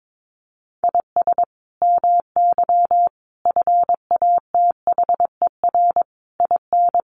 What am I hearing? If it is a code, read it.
Morse code, heard as a keyed tone, about 22 wpm, 700 Hz. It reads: IS MY FATHER IN
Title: Sentences from Top 500 Words 22wpm